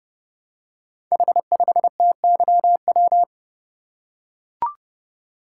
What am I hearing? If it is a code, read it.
Morse code, heard as a keyed tone, about 30 wpm, 700 Hz. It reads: H5TYW
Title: Code Groups - 5 Letter:Numbers - 30wpm